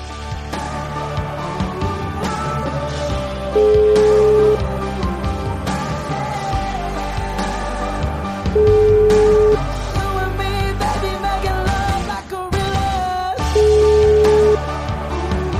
Ringback tone (Europe, 425 Hz), with noise.